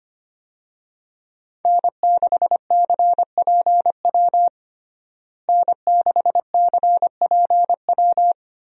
Morse code, heard as a keyed tone, about 25 wpm, 700 Hz. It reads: N6CPW N6CPW
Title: USA Call Signs - 5 Characters - Repeated 2x - 25wpm